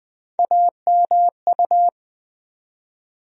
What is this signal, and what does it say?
Morse code, heard as a keyed tone, about 21 wpm, 700 Hz. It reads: AMU
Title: USA Call Sign Postfixes - 3 Letters 20wpm